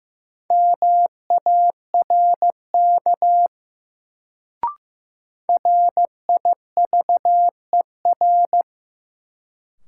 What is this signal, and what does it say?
Morse code, heard as a keyed tone, about 15 wpm, 700 Hz. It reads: MARK RIVER
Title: Sentences from Top 300 Words - New Word Drill - 15wpm